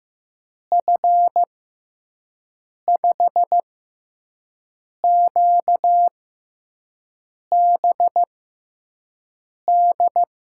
Morse code, heard as a keyed tone, about 15 wpm, spaced as for 5 wpm, 700 Hz. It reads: F5QBD
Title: USA Call Signs - 6x Character Spacing - 15wpm